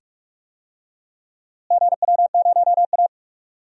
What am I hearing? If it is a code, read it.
Morse code, heard as a keyed tone, about 46 wpm, 700 Hz. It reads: GW0A
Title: DXCC-100 Call Signs - 4 Characters - Repeated 2x - 45wpm